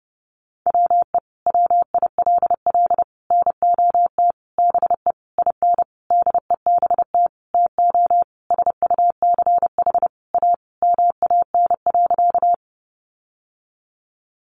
Morse code, heard as a keyed tone, about 30 wpm, 700 Hz. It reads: WE WILL NOT BE IN DEBT TO SUCH A MAN.
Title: Sentences from Top 1300 Words 30wpm